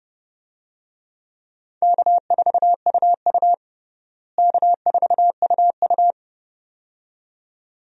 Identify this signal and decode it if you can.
Morse code, heard as a keyed tone, about 29 wpm, 700 Hz. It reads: K4UU K4UU
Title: USA Call Signs - 4 Characters - Repeated 2x - 30wpm